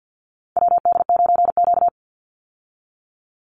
Morse code, heard as a keyed tone, about 50 wpm, 700 Hz. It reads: WD9Q
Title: Call Signs Repeated 3x - 50wpm